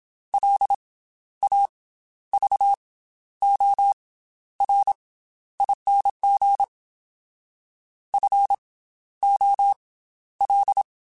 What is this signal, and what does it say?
Morse code, heard as a keyed tone, about 27 wpm, spaced as for 12 wpm, 800 Hz. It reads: LAVORING FOL
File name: Special INGs / Special INGs-mp3s / Slow ing @25